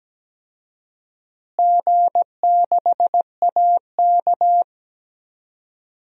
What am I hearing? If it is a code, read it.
Morse code, heard as a keyed tone, about 17 wpm, 700 Hz. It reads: G6AK
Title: DXCC-100 Call Signs - 4 Characters - Repeated 3x - 17wpm